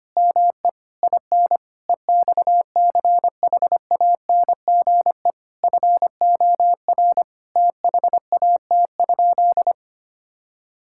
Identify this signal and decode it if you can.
Morse code, heard as a keyed tone, about 25 wpm, 700 Hz. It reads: ME IN EXCHANGE FOR THAT?
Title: Sentences from Top 1000 Words - Review Part 6 - 25wpm